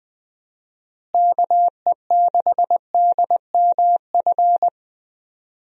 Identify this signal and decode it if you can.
Morse code, heard as a keyed tone, about 20 wpm, 700 Hz. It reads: KE6DMF
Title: USA Call Signs - 6 Characters - Repeated 3x - 20wpm